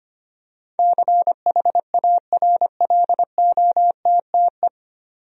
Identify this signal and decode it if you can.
Morse code, heard as a keyed tone, about 25 wpm, 700 Hz. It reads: CHARLOTTE
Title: USA Common Names - 25wpm